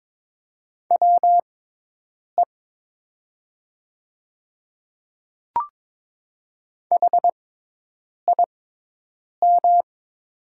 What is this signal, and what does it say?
Morse code, heard as a keyed tone, about 22 wpm, spaced as for 8 wpm, 700 Hz. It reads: WE HIM
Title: 6x Character Spacing - Top 100 Words - 22wpm